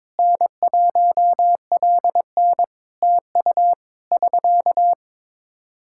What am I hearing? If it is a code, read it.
Morse code, heard as a keyed tone, about 22 wpm, 700 Hz. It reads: N1LN TU <SK>
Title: Practice QSO - Exchange 4 - Ending - 22wpm